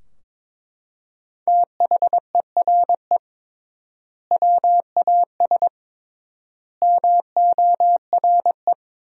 Morse code, heard as a keyed tone, about 22 wpm, 700 Hz. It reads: THERE WAS MORE
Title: 3x Word-Spacing - Sentences from Top 500 Words - 22wpm